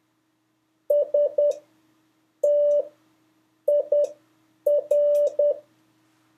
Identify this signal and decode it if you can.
Morse code, heard as a keyed tone, about 10 wpm, spaced as for 7 wpm, 580 Hz. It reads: STIR
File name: 3of5-words-2